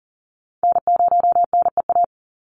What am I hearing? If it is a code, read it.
Morse code, heard as a keyed tone, about 40 wpm, 700 Hz. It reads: N0NEA
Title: USA Call Signs 40wpm